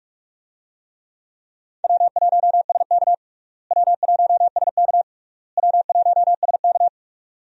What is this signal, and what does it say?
Morse code, heard as a keyed tone, about 46 wpm, 700 Hz. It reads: W1SK W1SK W1SK
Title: USA Call Signs - 4 Characters - Repeated 3x - 45wpm